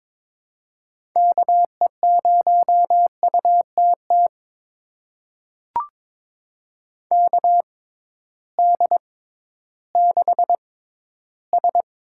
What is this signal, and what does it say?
Morse code, heard as a keyed tone, about 22 wpm, spaced as for 13 wpm, 700 Hz. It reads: KE0UTT KD6S
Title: USA Call Signs - 6 Characters - 6x Character Spacing - 22wpm